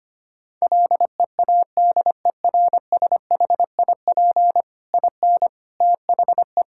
Morse code, heard as a keyed tone, about 25 wpm, 700 Hz. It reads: LEADERSHIP IN THE
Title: Sentences from Top 1600 Words 25wpm